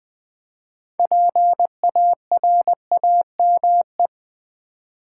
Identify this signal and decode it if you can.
Morse code, heard as a keyed tone, about 20 wpm, 700 Hz. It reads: PARAME
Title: Build A Word - 3 Letters At A Time - 9-Letter Words - 20wpm